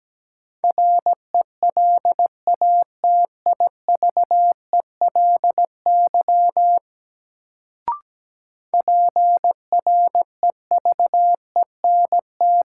Morse code, heard as a keyed tone, about 17 wpm, 700 Hz. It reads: RELATIVELY PREVENT
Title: Sentences from Top 1100 Words - New Word Drill Alt - 17wpm